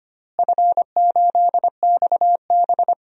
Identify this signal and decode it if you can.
Morse code, heard as a keyed tone, about 25 wpm, 700 Hz. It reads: F8XB
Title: DXCC-100 Call Signs - 4 Characters - Repeated 3x - 25wpm